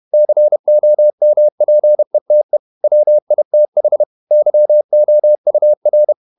Morse code, heard as a keyed tone, about 31 wpm, 600 Hz. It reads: COMPETE WITH YOUR
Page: Never 31wpm | CW med Gnister